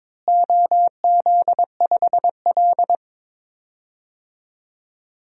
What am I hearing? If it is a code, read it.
Morse code, heard as a keyed tone, about 22 wpm, 700 Hz. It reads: OZ5L
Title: DXCC-100 Call Signs - 4 Characters - Repeated 3x - 22wpm